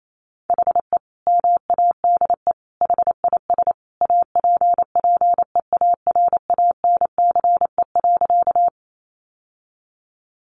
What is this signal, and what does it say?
Morse code, heard as a keyed tone, about 28 wpm, 700 Hz. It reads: HE MADE HIS APPEARANCE.
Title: Sentences from Top 1100 Words 28wpm